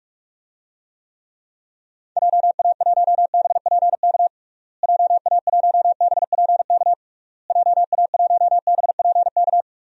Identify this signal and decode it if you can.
Morse code, heard as a keyed tone, about 42 wpm, 700 Hz. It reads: JA1BPK JA1BPK JA1BPK
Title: DXCC-100 Call Signs - 6 Characters - Repeated 3x - 45wpm